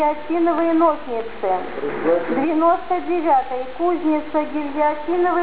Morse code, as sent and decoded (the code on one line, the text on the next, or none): none